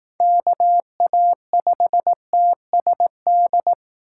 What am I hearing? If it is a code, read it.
Morse code, heard as a keyed tone, about 19 wpm, 700 Hz. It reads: KA5TSD
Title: Speed Racing - USA 2x3 Call Signs 15wpm